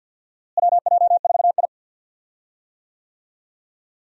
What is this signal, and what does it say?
Morse code, heard as a keyed tone, about 49 wpm, 700 Hz. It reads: WJ4I
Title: USA Call Signs - 4 Characters - Repeated 3x - 50wpm